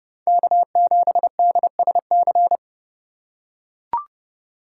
Morse code, heard as a keyed tone, about 29 wpm, 700 Hz. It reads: K7DSC